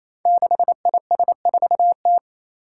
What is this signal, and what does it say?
Morse code, heard as a keyed tone, about 27 wpm, 700 Hz. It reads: 6IS4T
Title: Five Challenging Letter-Number Combinations 28wpm